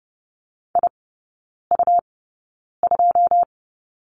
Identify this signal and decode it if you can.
Morse code, heard as a keyed tone, about 30 wpm, spaced as for 9 wpm, 690 Hz. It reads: IU2
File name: Lesson 17e - Letter u - Three Characters 30wpm